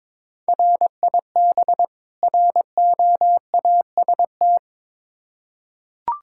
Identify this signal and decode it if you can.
Morse code, heard as a keyed tone, about 21 wpm, 700 Hz. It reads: RIB ROAST
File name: Difficult Words with Hints - 22wpm